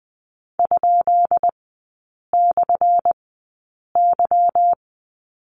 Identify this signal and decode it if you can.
Morse code, heard as a keyed tone, about 20 wpm, spaced as for 8 wpm, 700 Hz. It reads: ?/Y
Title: 178 Lesson 29e - Slash - Three Characters 20wpm